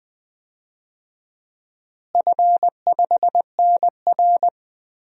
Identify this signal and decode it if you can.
Morse code, heard as a keyed tone, about 20 wpm, 700 Hz. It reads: F5NR
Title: DXCC-100 Call Signs - 4 Characters - Repeated 3x - 20wpm